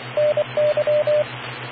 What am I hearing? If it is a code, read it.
Morse code, heard as a keyed tone, about 24 wpm, 600 Hz, with radio noise. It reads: NY